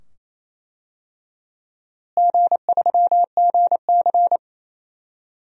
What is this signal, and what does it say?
Morse code, heard as a keyed tone, about 28 wpm, 700 Hz. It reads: G3GC
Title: DXCC-100 Call Signs - 4 Characters - Repeated 2x - 28wpm